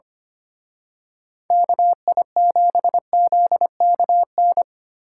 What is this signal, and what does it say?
Morse code, heard as a keyed tone, about 29 wpm, 700 Hz. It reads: KI7ZKN
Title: Speed Racing - USA Call Signs 25wpm